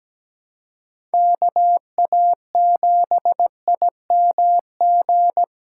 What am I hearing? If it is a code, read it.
Morse code, heard as a keyed tone, about 17 wpm, 700 Hz. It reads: KA7IMG
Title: USA Callsign DE Callsign - 6 Characters - 17wpm